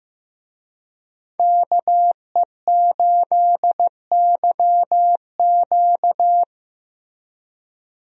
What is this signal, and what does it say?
Morse code, heard as a keyed tone, about 15 wpm, 700 Hz. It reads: KE8YQ
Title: USA Call Signs - 5 Characters - Repeated 2x - 15wpm